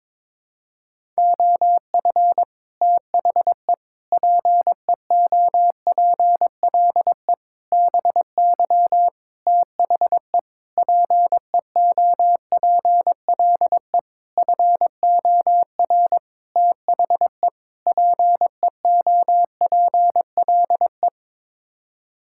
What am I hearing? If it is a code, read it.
Morse code, heard as a keyed tone, about 22 wpm, 700 Hz. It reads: OF THE PEOPLE BY THE PEOPLE FOR THE PEOPLE
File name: Sets of 9 Words - 22wpm